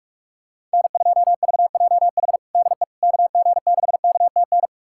Morse code, heard as a keyed tone, about 42 wpm, 690 Hz. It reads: N2VJH DE KG6KTN
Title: USA Callsign DE Callsign 45wpm